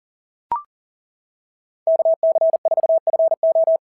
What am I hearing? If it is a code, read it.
Morse code, heard as a keyed tone, about 40 wpm, 690 Hz. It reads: KC4FO
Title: SOTA QSO Practice 40wpm